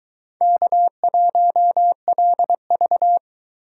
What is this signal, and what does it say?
Morse code, heard as a keyed tone, about 21 wpm, 700 Hz. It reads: K1LV